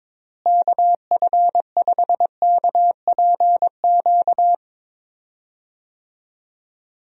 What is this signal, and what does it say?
Morse code, heard as a keyed tone, about 23 wpm, 700 Hz. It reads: KF5KPQ
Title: USA Call Signs - 6 Characters - Repeated 2x - 22wpm